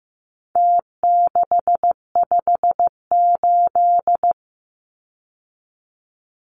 Morse code, heard as a keyed tone, about 15 wpm, 700 Hz. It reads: T658
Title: Numbers - Contest Style - 4-Digit 15wpm